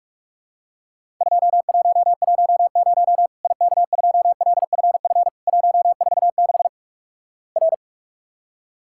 Morse code, heard as a keyed tone, about 46 wpm, 700 Hz. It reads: 2110 IK2LFF 146 R
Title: Worked All Europe DX Contest - QTC Exchange - 45wpm